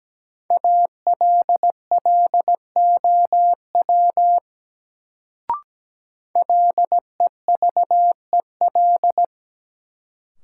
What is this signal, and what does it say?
Morse code, heard as a keyed tone, about 17 wpm, 700 Hz. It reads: ALLOW LEVEL
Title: Sentences from Top 700 Words - New Word Drill - 17wpm